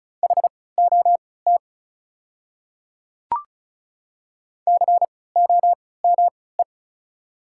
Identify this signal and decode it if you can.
Morse code, heard as a keyed tone, about 36 wpm, spaced as for 20 wpm, 690 Hz. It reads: HOT COME
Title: 3x Character Spacing - Top 100 Words - 35wpm